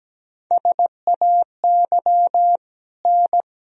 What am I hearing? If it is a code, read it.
Morse code, heard as a keyed tone, about 17 wpm, 700 Hz. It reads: SAY N
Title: Sentences from Top 200 Words and Abbreviations 17wpm